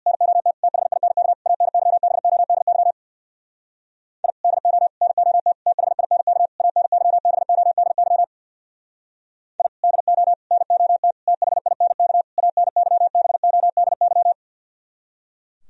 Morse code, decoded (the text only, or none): NOT THINK ANYBODY I DO NOT THINK ANYBODY I DO NOT THINK ANYBODY